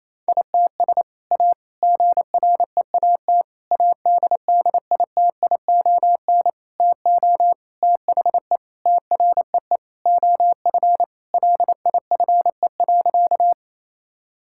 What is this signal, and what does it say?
Morse code, heard as a keyed tone, about 28 wpm, 700 Hz. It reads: ITS A GREAT ADDITION TO THE TREE OF LIFE.
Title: Sentences from Top 1000 Words - Review Part 6 - 28wpm